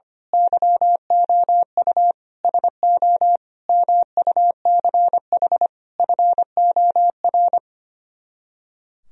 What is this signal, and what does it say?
Morse code, heard as a keyed tone, about 24 wpm, 700 Hz. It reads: YOU SO MUCH FOR
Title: Speed-Racing - Sets of 5 Words 25wpm